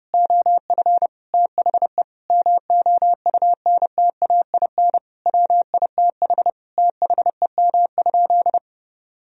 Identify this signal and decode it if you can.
Morse code, heard as a keyed tone, about 30 wpm, 700 Hz. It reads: OF THE MOUNTAIN WITH THEM?
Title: Sentences from Top 1000 Words - Review Part 3 - 30wpm